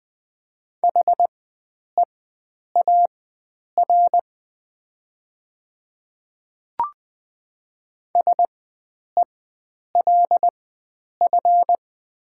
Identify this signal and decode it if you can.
Morse code, heard as a keyed tone, about 20 wpm, spaced as for 9 wpm, 700 Hz. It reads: HEAR SELF
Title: 4x Character Spacing - Top 300 Words - 20wpm